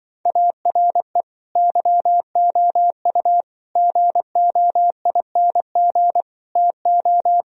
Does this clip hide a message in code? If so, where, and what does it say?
Yes: Morse code from 0.3 to 7.4 s, heard as a keyed tone, about 25 wpm, 700 Hz. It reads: ARE YOU GOING TO